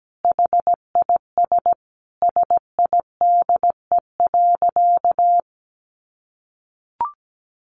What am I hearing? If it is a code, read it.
Morse code, heard as a keyed tone, about 17 wpm, 700 Hz. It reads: HIS SIDE.
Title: Sentences from Top 400 Words 17wpm